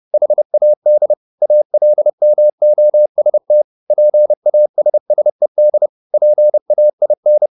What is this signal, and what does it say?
Morse code, heard as a keyed tone, about 30 wpm, 600 Hz. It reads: HAD ALMOST PASSED PAIN